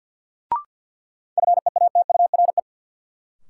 Morse code, heard as a keyed tone, about 52 wpm, 700 Hz. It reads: FEATURE
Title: Sentences from Top 800 Words - New Word Drill - 50wpm